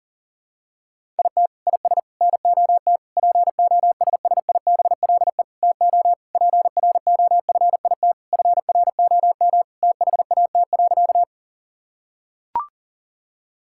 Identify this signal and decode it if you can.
Morse code, heard as a keyed tone, about 40 wpm, 700 Hz. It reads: IT IS NOT POSSIBLE TO PROFIT FROM THAT.
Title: Sentences from Top 1000 Words - Review Part 5 - 40wpm